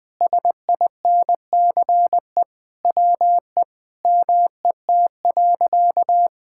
Morse code, heard as a keyed tone, about 21 wpm, 700 Hz. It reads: SINCE WE MET.
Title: Sentences with 4-Word Phrases - Spoken Beginning - 20wpm